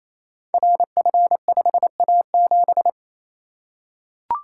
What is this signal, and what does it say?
Morse code, heard as a keyed tone, about 28 wpm, 700 Hz. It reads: RF5A7
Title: Code Groups - 5 Letter:Numbers - 28wpm